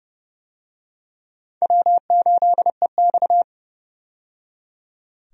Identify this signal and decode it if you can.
Morse code, heard as a keyed tone, about 29 wpm, 700 Hz. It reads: W8EX
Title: USA Call Signs - 4 Characters - 30wpm